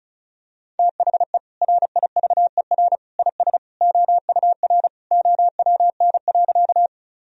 Morse code, heard as a keyed tone, about 36 wpm, 700 Hz. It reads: THE RIVER IS OUR OWN.
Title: Sentences from Top 1000 Words - Review Part 1 - 35wpm